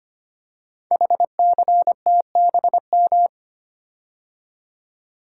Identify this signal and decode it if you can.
Morse code, heard as a keyed tone, about 25 wpm, 700 Hz. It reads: HCTBM
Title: Code Groups - 5 Letters - 25wpm